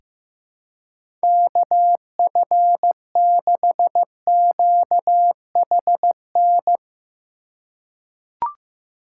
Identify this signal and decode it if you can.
Morse code, heard as a keyed tone, about 15 wpm, 700 Hz. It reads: KF6QHN
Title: USA Call Signs - 2x Character Spacing - 15wpm